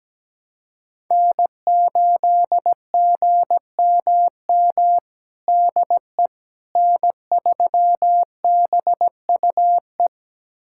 Morse code, heard as a keyed tone, about 17 wpm, 700 Hz. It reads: N8GMM DE N3BUE
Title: USA Callsign DE Callsign - 5 Characters - 17wpm